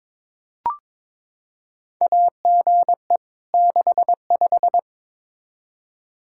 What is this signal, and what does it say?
Morse code, heard as a keyed tone, about 22 wpm, 700 Hz. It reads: AGE 65
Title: K1USN SST QSO Practice 22wpm